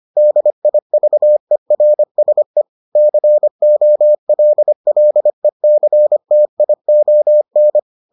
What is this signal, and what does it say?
Morse code, heard as a keyed tone, about 25 wpm, 600 Hz. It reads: DIVERSE COLLECTION